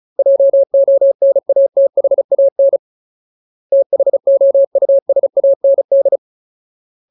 Morse code, heard as a keyed tone, about 35 wpm, 550 Hz. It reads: JONATHAN THOUSAND
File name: Ultimate Word List / 35WPM Ultimate